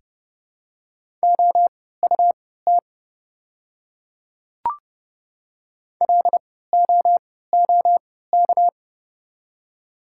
Morse code, heard as a keyed tone, about 31 wpm, spaced as for 17 wpm, 700 Hz. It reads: OUT LOOK
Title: 3x Character Spacing - Top 100 Words - 30wpm